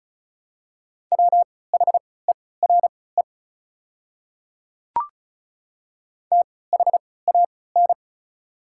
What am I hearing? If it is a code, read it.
Morse code, heard as a keyed tone, about 36 wpm, spaced as for 20 wpm, 690 Hz. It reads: WHERE THAN
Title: 3x Character Spacing - Top 200 Words - 35wpm